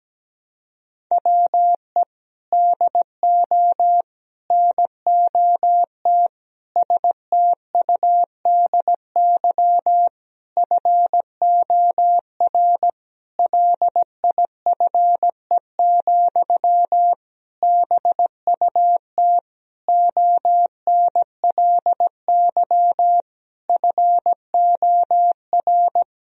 Morse code, heard as a keyed tone, about 17 wpm, 700 Hz. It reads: WE DO NOT STUDY FOR LIFE, BUT ONLY FOR
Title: Sentences from Top 1000 Words - Review Part 1 - 17wpm